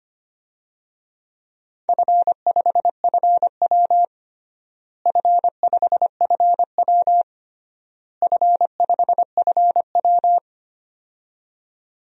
Morse code, heard as a keyed tone, about 25 wpm, 700 Hz. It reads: F5FW F5FW F5FW
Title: DXCC-100 Call Signs - 4 Characters - Repeated 3x - 25wpm